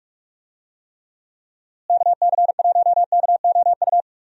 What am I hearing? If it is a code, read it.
Morse code, heard as a keyed tone, about 46 wpm, 700 Hz. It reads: KC1KOU
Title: USA Call Signs - 6 Characters - Repeated 3x - 45wpm